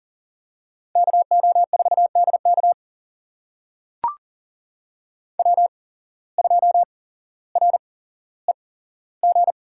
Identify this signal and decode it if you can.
Morse code, heard as a keyed tone, about 40 wpm, spaced as for 17 wpm, 700 Hz. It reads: KO4DK W2REG
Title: USA Call Signs - 5 Characters - 8x Character Spacing - 40wpm